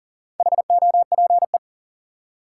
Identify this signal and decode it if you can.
Morse code, heard as a keyed tone, about 37 wpm, 700 Hz. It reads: HOPE
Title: Four Letter Words - 40wpm